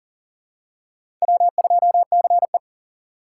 Morse code, heard as a keyed tone, about 37 wpm, 700 Hz. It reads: W2CE
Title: USA Call Signs - 2x Character Spacing - 40wpm